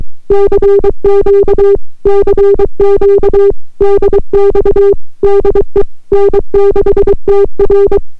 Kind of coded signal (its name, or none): Morse code